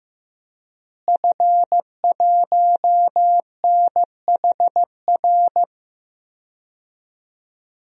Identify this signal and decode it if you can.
Morse code, heard as a keyed tone, about 15 wpm, 700 Hz. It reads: F1NHR
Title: DXCC-100 Call Signs - 5 Characters - Repeated 2x - 15wpm